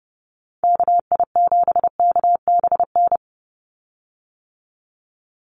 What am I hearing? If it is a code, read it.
Morse code, heard as a keyed tone, about 30 wpm, 700 Hz. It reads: KI7KBN
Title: USA Call Signs 30wpm